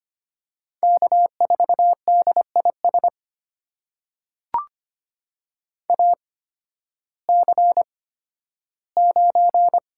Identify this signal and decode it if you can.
Morse code, heard as a keyed tone, about 25 wpm, spaced as for 13 wpm, 700 Hz. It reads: K4DIS AC9
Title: USA Call Signs - 8x Character Spacing - 25wpm